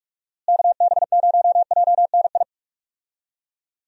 Morse code, heard as a keyed tone, about 42 wpm, 700 Hz. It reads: KB0JNI
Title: USA Call Signs - 6 Characters - Repeated 2x - 45wpm